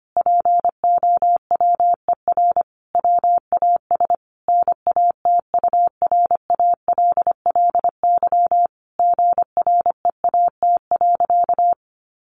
Morse code, heard as a keyed tone, about 25 wpm, 700 Hz. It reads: POWER WAS NATURALLY GREAT.
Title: Sentences from Top 1300 Words 25wpm